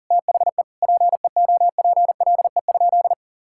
Morse code, heard as a keyed tone, about 40 wpm, 700 Hz. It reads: THE PEOPLE?
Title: Sentences from Top 1000 Words - Review Part 5 - 40wpm